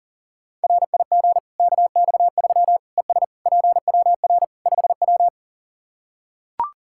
Morse code, heard as a keyed tone, about 40 wpm, 700 Hz. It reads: RIG KX3 ES PWR 5W
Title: Practice QSO - Exchange 2 - Info - 40wpm